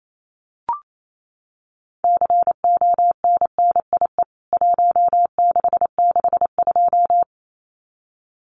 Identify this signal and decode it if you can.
Morse code, heard as a keyed tone, about 28 wpm, 700 Hz. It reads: CONNIE 1662
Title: CWT Exchange 28wpm